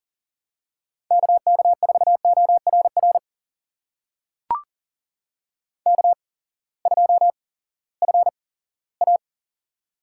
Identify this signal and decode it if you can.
Morse code, heard as a keyed tone, about 40 wpm, spaced as for 20 wpm, 700 Hz. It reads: KK4ORR K2FA
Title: USA Call Signs - 8x Character Spacing - 40wpm